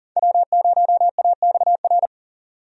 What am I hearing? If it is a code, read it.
Morse code, heard as a keyed tone, about 40 wpm, 700 Hz. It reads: W0AXR
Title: USA Call Signs - 5 Characters - 40wpm